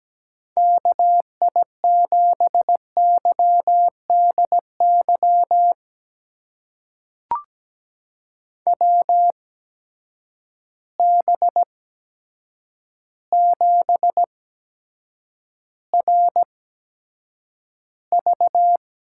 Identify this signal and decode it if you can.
Morse code, heard as a keyed tone, about 17 wpm, spaced as for 8 wpm, 700 Hz. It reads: KI7YDY WB7RV
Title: USA Call Signs - 6 Characters - 8x Character Spacing - 17wpm